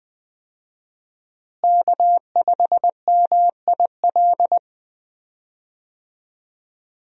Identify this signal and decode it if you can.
Morse code, heard as a keyed tone, about 20 wpm, 700 Hz. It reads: K5MIL
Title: USA Call Signs - 5 Characters - Repeated 2x - 20wpm